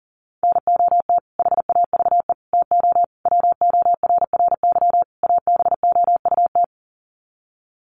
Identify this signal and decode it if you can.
Morse code, heard as a keyed tone, about 40 wpm, 700 Hz. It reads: NOT HAVE TO WORRY ABOUT
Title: Sets of 5 Words 40wpm